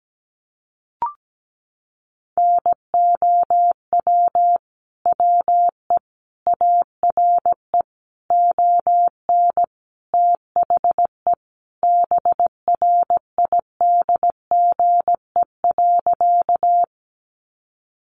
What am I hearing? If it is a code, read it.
Morse code, heard as a keyed tone, about 17 wpm, 700 Hz. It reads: NOW WE ARE ON THE BRIDGE.
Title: Sentences from Top 1200 Words 17wpm